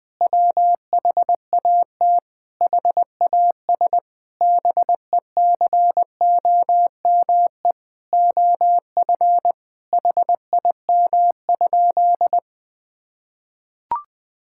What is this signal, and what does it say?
Morse code, heard as a keyed tone, about 20 wpm, 700 Hz. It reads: WHAT HAS BECOME OF HIM?
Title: Sentences from Top 1000 Words - Review Part 5 - 20wpm